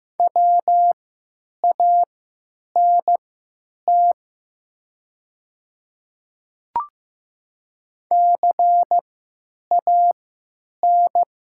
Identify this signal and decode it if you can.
Morse code, heard as a keyed tone, about 15 wpm, spaced as for 9 wpm, 700 Hz. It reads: WANT CAN
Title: 3x Character Spacing - Top 200 Words - 15wpm